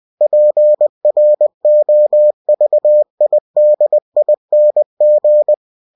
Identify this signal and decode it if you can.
Morse code, heard as a keyed tone, about 20 wpm, 600 Hz. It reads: PROVIDING
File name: cw-books / winnie-the-pooh / winnie-20wpm-page-0115